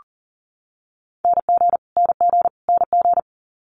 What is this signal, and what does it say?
Morse code, heard as a keyed tone, about 38 wpm, 700 Hz. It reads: NG NG NG
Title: Multi-letter Phonemes Repeated 3x 40wpm